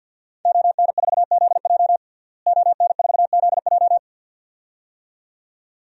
Morse code, heard as a keyed tone, about 48 wpm, 700 Hz. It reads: ON4ZJ ON4ZJ
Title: DXCC-100 Call Signs - 5 Characters - Repeated 2x - 50wpm